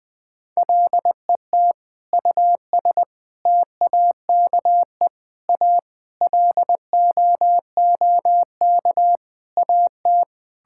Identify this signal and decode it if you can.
Morse code, heard as a keyed tone, about 20 wpm, 700 Hz. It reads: LET US TAKE A LOOK AT
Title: Sentences with 5-Word Phrases - Spoken Beginning - 20wpm